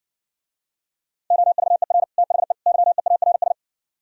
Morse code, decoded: OVER THE YEARS

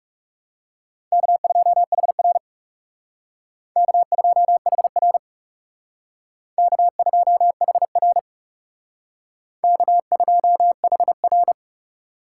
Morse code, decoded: K2HR K2HR K2HR K2HR

